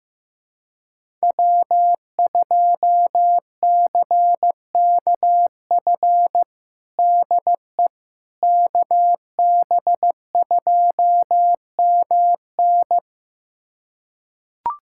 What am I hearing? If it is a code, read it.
Morse code, heard as a keyed tone, about 16 wpm, 700 Hz. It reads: W2CKF DE KB2MN
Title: USA Callsign DE Callsign - 5 Characters - 15wpm